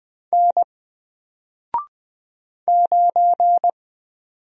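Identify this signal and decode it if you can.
Morse code, heard as a keyed tone, about 20 wpm, 700 Hz. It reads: N 9
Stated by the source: Single letter and number Morse drill at 20 wpm, from N4LCD.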